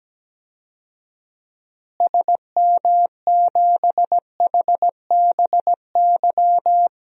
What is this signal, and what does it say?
Morse code, heard as a keyed tone, about 17 wpm, 700 Hz. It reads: SM7HBY
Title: DXCC-100 Call Signs - 6 Characters - Repeated 3x - 17wpm